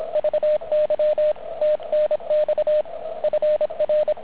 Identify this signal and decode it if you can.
Morse code, heard as a keyed tone, about 27 wpm, 610 Hz, with radio noise. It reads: VY TNX FR